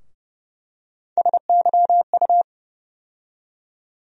Morse code, heard as a keyed tone, about 31 wpm, 700 Hz. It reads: SYU
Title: USA Call Sign Postfixes - 3 Letters 30wpm